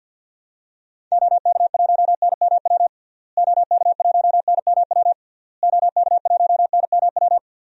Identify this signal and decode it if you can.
Morse code, heard as a keyed tone, about 46 wpm, 700 Hz. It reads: OK1NMW OK1NMW OK1NMW
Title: DXCC-100 Call Signs - 6 Characters - Repeated 3x - 50wpm